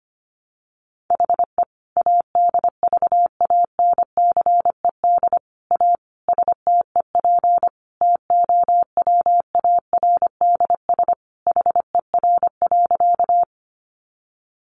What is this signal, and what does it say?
Morse code, heard as a keyed tone, about 25 wpm, 700 Hz. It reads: HE ADVANCED A STEP TOWARDS HER.
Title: Sentences from Top 1200 Words 25wpm